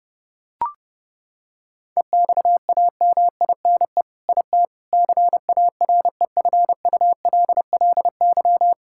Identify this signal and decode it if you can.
Morse code, heard as a keyed tone, about 30 wpm, 700 Hz. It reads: EXAMINE IT CAREFULLY
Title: Sentences from Top 1600 Words 30wpm